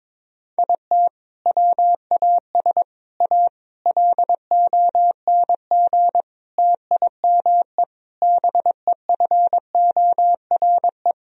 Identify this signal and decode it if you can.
Morse code, heard as a keyed tone, about 22 wpm, 700 Hz. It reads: IT WAS A LONG TIME BEFORE
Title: Sentences from Top 1000 Words - Review Part 2 - 22wpm